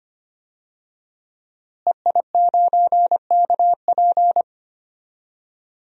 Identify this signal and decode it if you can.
Morse code, heard as a keyed tone, about 25 wpm, 700 Hz. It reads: EI9KP
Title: DXCC-100 Call Signs - 5 Characters - Repeated 2x - 25wpm